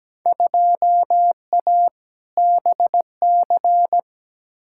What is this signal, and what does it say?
Morse code, heard as a keyed tone, about 17 wpm, 700 Hz. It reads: 2A BC
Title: Field Day Exchanges - 17wpm